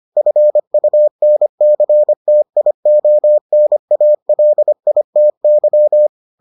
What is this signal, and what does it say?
Morse code, heard as a keyed tone, about 25 wpm, 600 Hz. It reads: FUNCTIONALITY